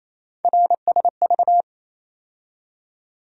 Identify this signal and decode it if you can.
Morse code, heard as a keyed tone, about 28 wpm, 700 Hz. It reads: RSV
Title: USA Call Sign Postfixes - 3 Letters 28wpm